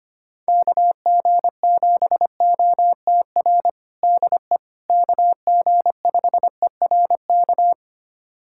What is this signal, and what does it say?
Morse code, heard as a keyed tone, about 25 wpm, 700 Hz. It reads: KG7OTR DE KG5ERK
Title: USA Callsign DE Callsign - 6 Characters - 25wpm